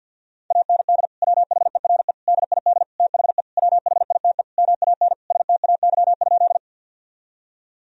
Morse code, heard as a keyed tone, about 48 wpm, 700 Hz. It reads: AND WHERE DID THE WHITE MAN STAY?
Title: Sentences from Top 1000 Words - Review Part 1 - 50wpm